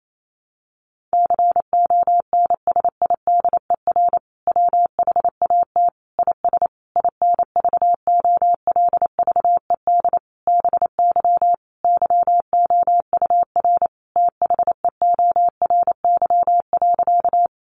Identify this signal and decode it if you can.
Morse code, heard as a keyed tone, about 28 wpm, 700 Hz. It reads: CONSIDER WHAT IS INVOLVED BY YOUR THEORY.
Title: Sentences from Top 800 Words 28wpm